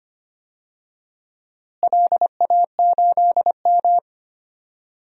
Morse code, heard as a keyed tone, about 25 wpm, 700 Hz. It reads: LA8M
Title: DXCC-100 Call Signs - 4 Characters - Repeated 2x - 25wpm